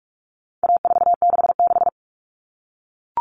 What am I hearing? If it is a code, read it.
Morse code, heard as a keyed tone, about 44 wpm, 700 Hz. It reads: A466
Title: Common Cut Numbers - 4-Digit 45wpm